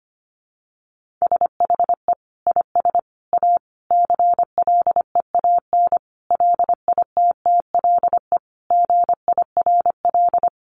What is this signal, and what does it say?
Morse code, heard as a keyed tone, about 25 wpm, 700 Hz. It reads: SHE IS A CLEAN LITTLE GIRL
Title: Sentences from Top 1100 Words 25wpm